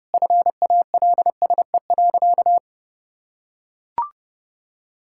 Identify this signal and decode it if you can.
Morse code, heard as a keyed tone, about 30 wpm, 700 Hz. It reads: FALSE.
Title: Sentences from Top 1600 Words 30wpm